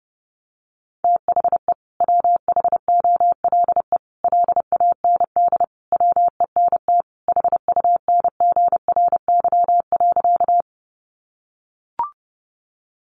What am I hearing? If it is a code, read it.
Morse code, heard as a keyed tone, about 30 wpm, 700 Hz. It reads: THE WHOLE LAND WENT HUNGRY.
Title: Sentences from Top 1400 Words 30wpm